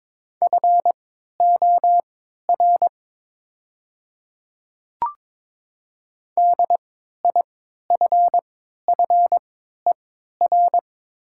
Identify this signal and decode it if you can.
Morse code, heard as a keyed tone, about 22 wpm, spaced as for 13 wpm, 700 Hz. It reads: FOR DIFFER
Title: 3x Character Spacing - Top 200 Words - 22wpm